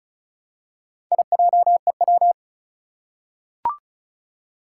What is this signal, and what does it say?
Morse code, heard as a keyed tone, about 36 wpm, 700 Hz. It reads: IJEW
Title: Four Challenging Letter-Number Combinations 35wpm